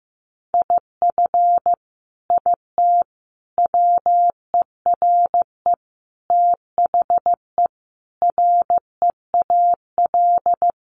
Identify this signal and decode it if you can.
Morse code, heard as a keyed tone, about 15 wpm, 700 Hz. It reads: IF IT WERE THE REAL
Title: Sentences from Top 1300 Words 15wpm